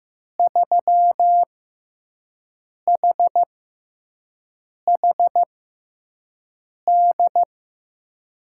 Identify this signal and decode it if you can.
Morse code, heard as a keyed tone, about 15 wpm, spaced as for 5 wpm, 700 Hz. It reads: 3HHD
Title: USA Call Signs - 6 Characters - 6x Character Spacing - 15wpm